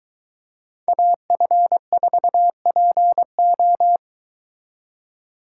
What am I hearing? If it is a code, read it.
Morse code, heard as a keyed tone, about 23 wpm, 700 Hz. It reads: AF4PO